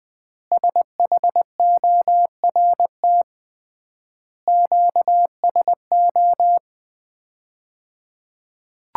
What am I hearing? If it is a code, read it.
Morse code, heard as a keyed tone, about 20 wpm, 700 Hz. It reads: SHORT QSO
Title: Common QSO Phrases - 3x Word-Spacing - 20wpm